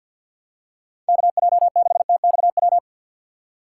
Morse code, heard as a keyed tone, about 53 wpm, 700 Hz. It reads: KJ6TXW
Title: USA Call Signs - 6 Characters - 6x Character Spacing - 50wpm